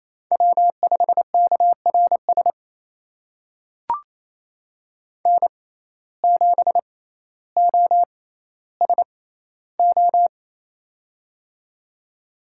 Morse code, heard as a keyed tone, about 27 wpm, spaced as for 14 wpm, 700 Hz. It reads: W5KRS N7OSO
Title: USA Call Signs - 5 Characters - 6x Character Spacing - 28wpm